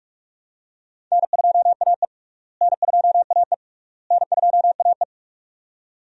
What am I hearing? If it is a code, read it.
Morse code, heard as a keyed tone, about 46 wpm, 700 Hz. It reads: N2AE N2AE N2AE
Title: USA Call Signs - 4 Characters - Repeated 3x - 45wpm